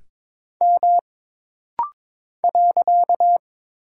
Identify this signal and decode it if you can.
Morse code, heard as a keyed tone, about 22 wpm, 700 Hz. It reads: M .
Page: This rapid-fire timing is geared to push your ICR (Instant Character Recognition) skills.